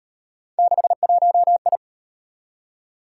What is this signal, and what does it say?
Morse code, heard as a keyed tone, about 38 wpm, 700 Hz. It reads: 61I